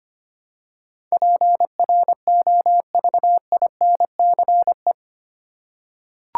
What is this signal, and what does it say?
Morse code, heard as a keyed tone, about 25 wpm, 700 Hz. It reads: PROVINCE
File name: Eight Letter Words - 25wpm